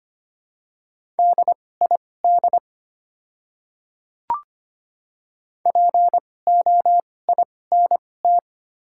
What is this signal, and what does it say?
Morse code, heard as a keyed tone, about 25 wpm, spaced as for 18 wpm, 700 Hz. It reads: DID POINT
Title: 2x Character Spacing - Top 200 Words - 25wpm